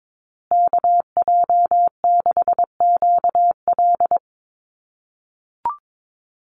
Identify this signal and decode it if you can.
Morse code, heard as a keyed tone, about 22 wpm, 700 Hz. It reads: KJ6QL
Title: USA Call Signs Repeated 3x 22wpm